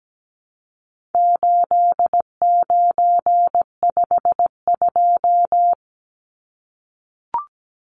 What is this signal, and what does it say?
Morse code, heard as a keyed tone, about 17 wpm, 700 Hz. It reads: 8952
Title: Numbers - 4-Digit 17wpm